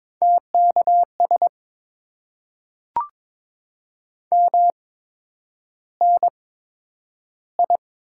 Morse code, heard as a keyed tone, about 23 wpm, spaced as for 10 wpm, 700 Hz. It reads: TKS MNI
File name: Top 90 QSO Elements - 8x Character Spacing - 22wpm